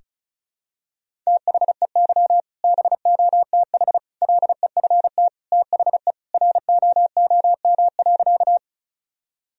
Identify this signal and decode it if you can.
Morse code, heard as a keyed tone, about 36 wpm, 700 Hz. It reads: THEY BOTH LEFT THE ROOM.
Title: Sentences from Top 1000 Words - Review Part 1 - 35wpm